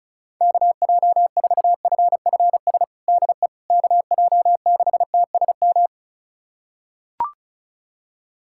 Morse code, heard as a keyed tone, about 34 wpm, 700 Hz. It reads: KJ4FFS DE KJ6TSM
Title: USA Callsign DE Callsign - 6 Characters - 35wpm